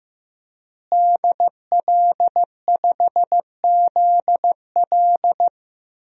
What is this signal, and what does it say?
Morse code, heard as a keyed tone, about 15 wpm, 700 Hz. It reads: DL5ZL
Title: DXCC-100 Call Signs - 5 Characters - Repeated 3x - 15wpm